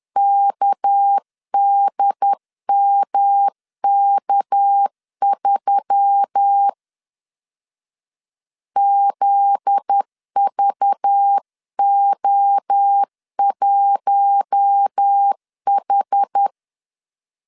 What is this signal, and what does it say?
Morse code, heard as a keyed tone, about 10 wpm, 800 Hz. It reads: KDMK3 ZVO1H